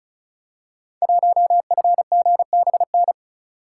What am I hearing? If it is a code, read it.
Morse code, heard as a keyed tone, about 36 wpm, 700 Hz. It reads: 1FGBN
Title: Code Groups - 5 Letter:Numbers - 35wpm